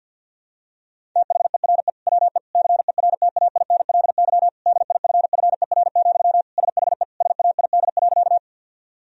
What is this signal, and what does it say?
Morse code, heard as a keyed tone, about 52 wpm, 700 Hz. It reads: THERE WE CERTAINLY DIFFER, SHE SAID.
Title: Sentences from Top 1000 Words - Review Part 1 - 50wpm